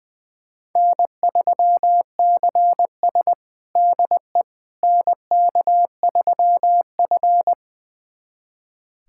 Morse code, heard as a keyed tone, about 20 wpm, 700 Hz. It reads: N3CS DE NK3F
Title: USA Callsign DE Callsign - 4 Characters - 20wpm